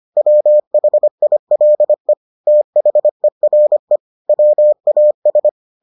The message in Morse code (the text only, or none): WHILE THERE WAS